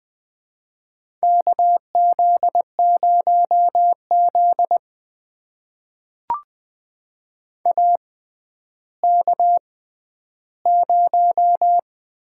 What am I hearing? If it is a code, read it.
Morse code, heard as a keyed tone, about 20 wpm, spaced as for 11 wpm, 700 Hz. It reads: KZ0Z AK0
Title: USA Call Signs - 4 Characters - 6x Character Spacing - 20wpm